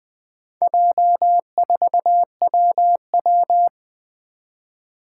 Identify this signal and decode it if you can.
Morse code, heard as a keyed tone, about 20 wpm, 700 Hz. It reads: J4WW
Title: Four Challenging Letter-Number Combinations 20wpm